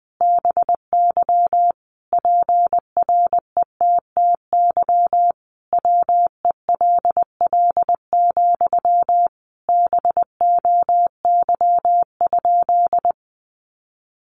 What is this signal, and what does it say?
Morse code, heard as a keyed tone, about 19 wpm, 700 Hz. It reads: BY PRETTY WELL, BOY?
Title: Sentences from Top 1000 Words 20wpm